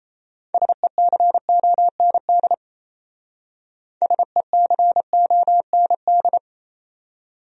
Morse code, heard as a keyed tone, about 32 wpm, 700 Hz. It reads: SECOND SECOND
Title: Speed-Racing Top 500 Words 28wpm